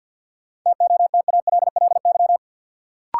Morse code, heard as a keyed tone, about 48 wpm, 700 Hz. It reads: TOTALLY
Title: Sentences from Top 1600 Words - New Word Drill - 50wpm